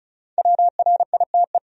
Morse code, heard as a keyed tone, about 36 wpm, 700 Hz. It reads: WRITE
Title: Top 100 Words 35wpm